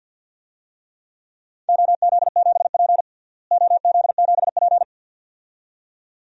Morse code, heard as a keyed tone, about 52 wpm, 700 Hz. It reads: OZ7P OZ7P
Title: DXCC-100 Call Signs - 4 Characters - Repeated 2x - 50wpm